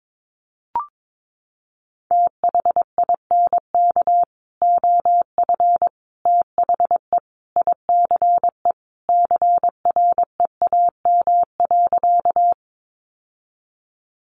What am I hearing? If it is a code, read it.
Morse code, heard as a keyed tone, about 22 wpm, 700 Hz. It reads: THINK OF THE ICE CREAM.
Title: Sentences from Top 1500 Words 22wpm